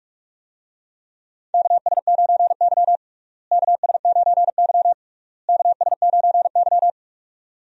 Morse code, heard as a keyed tone, about 46 wpm, 700 Hz. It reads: KS9Y KS9Y KS9Y
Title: USA Call Signs - 4 Characters - Repeated 3x - 45wpm